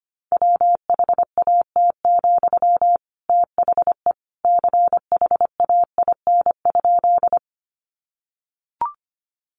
Morse code, heard as a keyed tone, about 25 wpm, 700 Hz. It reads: WHAT, THE CHAIN?
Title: Sentences from Top 1500 Words 25wpm